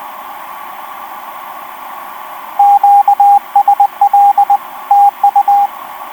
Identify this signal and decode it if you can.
Morse code, heard as a keyed tone, about 20 wpm, 810 Hz, with radio noise. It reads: QSL TU
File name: v5-dk1ce f4ffh etc 10m cw 05jan2011 1458